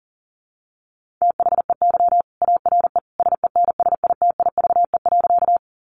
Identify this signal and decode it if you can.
Morse code, heard as a keyed tone, about 40 wpm, 700 Hz. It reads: THEY ARE SENSITIVE.
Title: Sentences from Top 1400 Words 40wpm